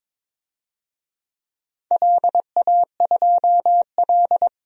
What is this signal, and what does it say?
Morse code, heard as a keyed tone, about 22 wpm, 700 Hz. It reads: LA2L
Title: DXCC-100 Call Signs - 4 Characters - 22wpm